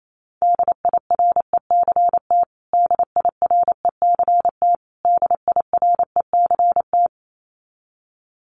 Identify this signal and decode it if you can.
Morse code, heard as a keyed tone, about 28 wpm, 700 Hz. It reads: DIRECT DIRECT DIRECT
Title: Top 500 Words Repeated 28wpm